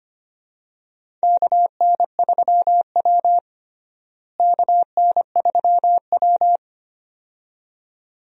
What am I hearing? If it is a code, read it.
Morse code, heard as a keyed tone, about 25 wpm, 700 Hz. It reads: KN3W KN3W
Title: USA Call Signs - 4 Characters - Repeated 2x - 25wpm